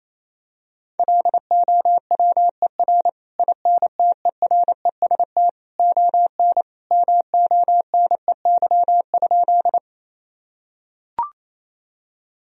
Morse code, heard as a keyed tone, about 28 wpm, 700 Hz. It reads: LOWER INTEREST ON MONEY?
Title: Sentences from Top 1000 Words - Review Part 4 - 28wpm